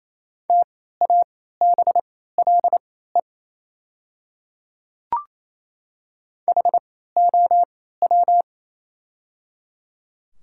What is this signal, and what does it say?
Morse code, heard as a keyed tone, about 28 wpm, spaced as for 16 wpm, 700 Hz. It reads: TABLE HOW
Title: 3x Character Spacing - Top 500 Words - 28wpm